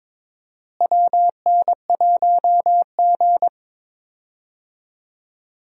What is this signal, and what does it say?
Morse code, heard as a keyed tone, about 22 wpm, 700 Hz. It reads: WN1G
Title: USA Call Signs - 4 Characters - Repeated 3x - 22wpm